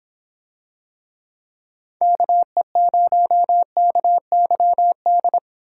Morse code, heard as a keyed tone, about 26 wpm, 700 Hz. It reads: KE0KYD